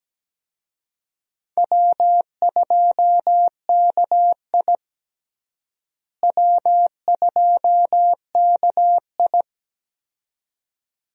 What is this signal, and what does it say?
Morse code, heard as a keyed tone, about 17 wpm, 700 Hz. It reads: W2KI W2KI
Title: USA Call Signs - 4 Characters - Repeated 2x - 17wpm